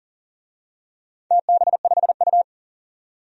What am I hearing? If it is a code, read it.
Morse code, heard as a keyed tone, about 39 wpm, 700 Hz. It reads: TB5U
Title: DXCC-100 Call Signs - 4 Characters - 40wpm